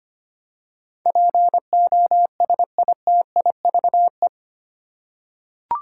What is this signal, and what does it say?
Morse code, heard as a keyed tone, about 25 wpm, 700 Hz. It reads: POSITIVE
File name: 8 Letter Words - First 3 Letters Given - 25wpm